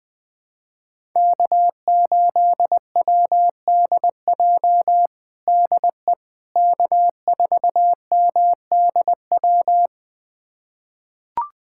Morse code, heard as a keyed tone, about 20 wpm, 700 Hz. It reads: K8WDJ DE K4MDW
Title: USA Callsign DE Callsign - 5 Characters - 20wpm